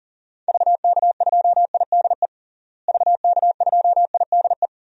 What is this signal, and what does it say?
Morse code, heard as a keyed tone, about 37 wpm, 700 Hz. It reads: VK2IDE VK2IDE
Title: DXCC-100 Call Signs - 6 Characters - Repeated 2x - 40wpm